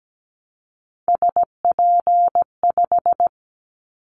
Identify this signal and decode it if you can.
Morse code, heard as a keyed tone, about 17 wpm, 700 Hz. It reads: SP5
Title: Callsign Prefixes for Top 25 DXCC - 17wpm